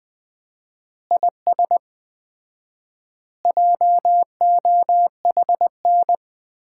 Morse code, heard as a keyed tone, about 21 wpm, 700 Hz. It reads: IS JOHN
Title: Common QSO Phrases - 4x Word-Spacing - 20wpm